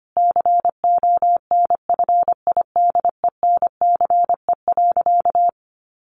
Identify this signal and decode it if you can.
Morse code, heard as a keyed tone, about 25 wpm, 700 Hz. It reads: CONFIDENCE.
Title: Sentences from Top 1500 Words 25wpm